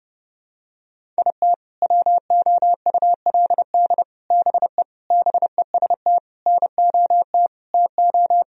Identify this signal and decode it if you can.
Morse code, heard as a keyed tone, about 30 wpm, 700 Hz. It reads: IT WOULD BE BEST NOT TO
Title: Sentences from Top 1000 Words - Review Part 1 - 30wpm